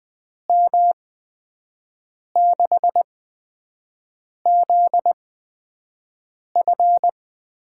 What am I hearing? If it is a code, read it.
Morse code, heard as a keyed tone, about 20 wpm, spaced as for 5 wpm, 700 Hz. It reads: M6ZF
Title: USA Call Signs - 6 Characters - 8x Character Spacing - 20wpm